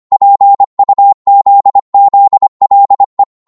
Morse code, heard as a keyed tone, about 25 wpm, 800 Hz. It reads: PUZZLE